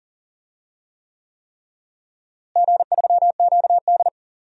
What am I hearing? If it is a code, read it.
Morse code, heard as a keyed tone, about 40 wpm, 700 Hz. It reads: G3QD
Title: DXCC-100 Call Signs - 4 Characters - Repeated 3x - 40wpm